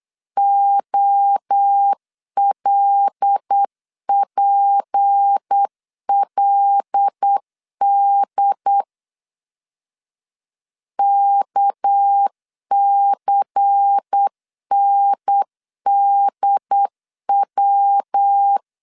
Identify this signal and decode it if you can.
Morse code, heard as a keyed tone, about 8 wpm, 800 Hz. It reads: OLPLD KCNDW
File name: LEZIONI / 6 PKX0 / 6 PK 8wpm 3003